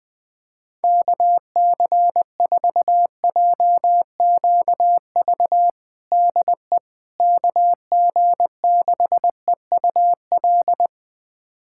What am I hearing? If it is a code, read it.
Morse code, heard as a keyed tone, about 21 wpm, 700 Hz. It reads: KC4JQV DE KG6EUL
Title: USA Callsign DE Callsign - 6 Characters - 20wpm